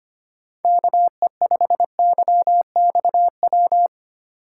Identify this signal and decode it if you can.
Morse code, heard as a keyed tone, about 31 wpm, 700 Hz. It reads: KE5YXW
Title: Speed Racing - USA 2x3 Call Signs 25wpm